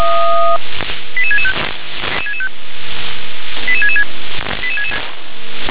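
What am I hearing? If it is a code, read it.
Five-tone selective call (ZVEI), heard as quick runs of five tones, tones of about 70 ms. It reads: 88665 88666
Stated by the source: Five-tone ZVEI selective call received by SDR.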